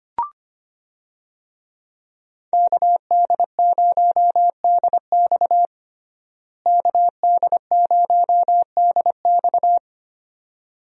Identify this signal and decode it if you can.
Morse code, heard as a keyed tone, about 25 wpm, 700 Hz. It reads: KD0DX KD0DX
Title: USA Call Signs - 5 Characters - Repeated 2x - 25wpm